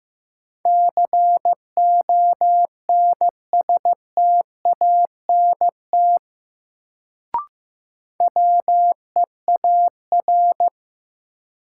Morse code, heard as a keyed tone, about 15 wpm, 700 Hz. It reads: CONSTANT WEAR
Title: Sentences from Top 1500 Words - New Word Drill Alt - 15wpm